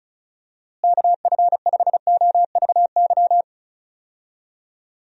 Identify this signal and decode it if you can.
Morse code, heard as a keyed tone, about 34 wpm, 700 Hz. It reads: KF5OVY
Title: USA Call Signs - 6 Characters - Repeated 2x - 35wpm